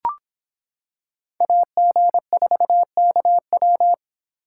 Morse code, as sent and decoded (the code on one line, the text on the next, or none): .- --. ....- -.- .--
AG4KW